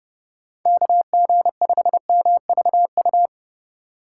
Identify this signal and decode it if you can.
Morse code, heard as a keyed tone, about 30 wpm, 700 Hz. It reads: KG5MVU
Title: USA Call Signs - 6 Characters - 4x Character Spacing - 30wpm